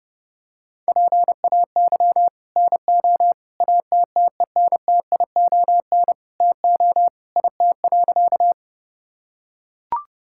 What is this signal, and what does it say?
Morse code, heard as a keyed tone, about 30 wpm, 700 Hz. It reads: PAY NO ATTENTION TO IT.
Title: Sentences from Top 1000 Words - Review Part 6 - 30wpm